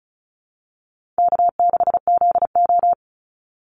Morse code, heard as a keyed tone, about 34 wpm, 700 Hz. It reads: K6ZO
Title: Call Signs - 35wpm